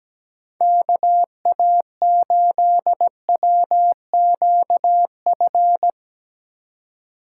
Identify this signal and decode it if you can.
Morse code, heard as a keyed tone, about 17 wpm, 700 Hz. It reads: KA8WQF
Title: USA Call Signs - 6 Characters - 2x Character Spacing - 17wpm